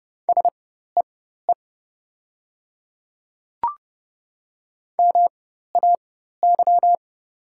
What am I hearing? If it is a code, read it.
Morse code, heard as a keyed tone, about 31 wpm, spaced as for 14 wpm, 700 Hz. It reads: SEE MAY
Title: 4x Character Spacing - Top 100 Words - 30wpm